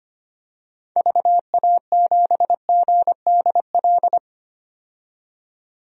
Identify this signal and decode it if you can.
Morse code, heard as a keyed tone, about 25 wpm, 690 Hz. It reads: VA7GDL
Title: DXCC-100 Call Signs - 6 Characters - 25wpm